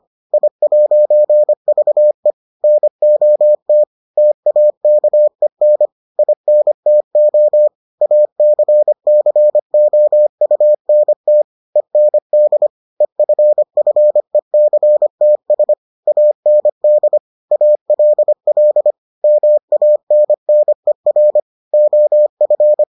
Morse code, heard as a keyed tone, about 25 wpm, 600 Hz. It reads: I'VE NOT TAKEN INTO ACCOUNT END EFFECTS AND ALL MANNER OF